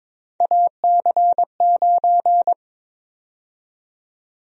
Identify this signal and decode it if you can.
Morse code, heard as a keyed tone, about 22 wpm, 700 Hz. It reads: AC9
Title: USA Call Sign Prefixes - 22wpm